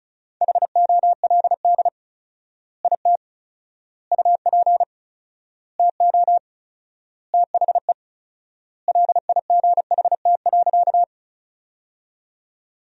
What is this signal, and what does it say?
Morse code, heard as a keyed tone, about 36 wpm, 700 Hz. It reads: HOLD IT UP TO THE LIGHT.
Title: 4x Word-Spacing - Sentences from Top 400 Words - 35wpm